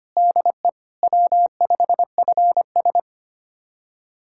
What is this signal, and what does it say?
Morse code, heard as a keyed tone, about 25 wpm, 700 Hz. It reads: DE W5FS
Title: USA Callsign DE Callsign - 4 Characters - 25wpm